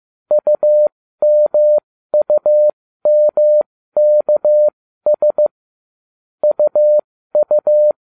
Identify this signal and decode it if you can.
Morse code, heard as a keyed tone, about 15 wpm, spaced as for 13 wpm, 600 Hz. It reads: UMUMKS UU